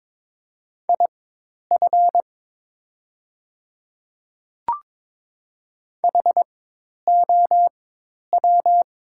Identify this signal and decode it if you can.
Morse code, heard as a keyed tone, about 22 wpm, spaced as for 10 wpm, 700 Hz. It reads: IF HOW
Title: 4x Character Spacing - Top 100 Words - 22wpm